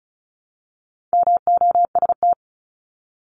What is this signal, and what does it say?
Morse code, heard as a keyed tone, about 34 wpm, 700 Hz. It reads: MOST
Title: LY Suffix - Top 100 Words with Base Words 35wpm